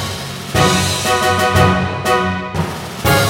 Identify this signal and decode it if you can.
There is no code.